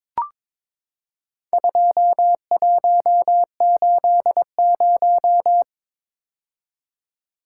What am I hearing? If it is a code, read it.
Morse code, heard as a keyed tone, about 23 wpm, 700 Hz. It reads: 2180
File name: practice / 22wpm / Four Letter-Number Combinations - 22wpm